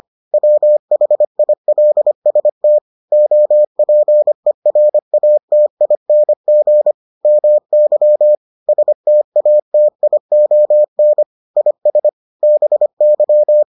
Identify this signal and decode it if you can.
Morse code, heard as a keyed tone, about 25 wpm, 600 Hz. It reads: WHILST OPERATING MY STATION IS BY